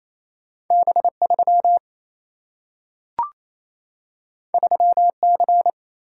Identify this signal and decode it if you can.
Morse code, heard as a keyed tone, about 28 wpm, 700 Hz. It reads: B 3 3 C
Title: Single-Letter Single-Number 28wpm